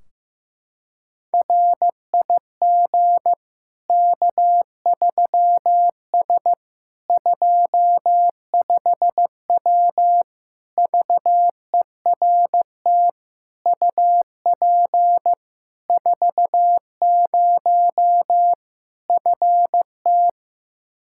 Morse code, heard as a keyed tone, about 15 wpm, 700 Hz. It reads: RIG K3S 25W VERT UP 40 FT
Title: K1USN SST QSO Practice 15wpm